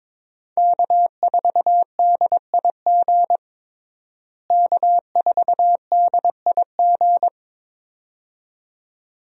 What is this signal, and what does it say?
Morse code, heard as a keyed tone, about 22 wpm, 700 Hz. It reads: K4DIG K4DIG
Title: USA Call Signs - 5 Characters - Repeated 2x - 22wpm